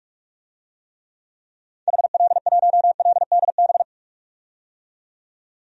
Morse code, heard as a keyed tone, about 46 wpm, 700 Hz. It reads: HL1LDB
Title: DXCC-100 Call Signs - 6 Characters - 45wpm